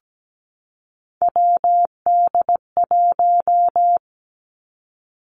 Morse code, heard as a keyed tone, about 17 wpm, 700 Hz. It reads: WD1
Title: Three Challenging Letter-Number Combinations 17wpm